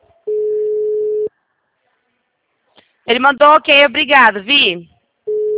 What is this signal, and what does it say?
Ringback tone (Europe, 425 Hz), with speech.